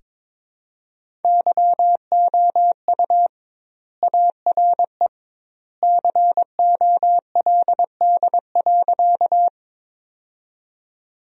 Morse code, heard as a keyed tone, about 21 wpm, 700 Hz. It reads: YOU ARE COLD.
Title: 2x Word-Spacing - Sentences from Top 400 Words - 22wpm